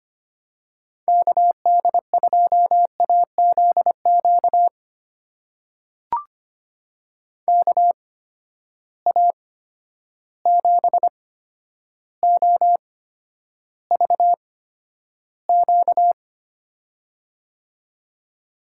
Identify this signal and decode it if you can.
Morse code, heard as a keyed tone, about 25 wpm, spaced as for 11 wpm, 700 Hz. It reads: KD2AZQ KA7OVQ
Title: USA Call Signs - 8x Character Spacing - 25wpm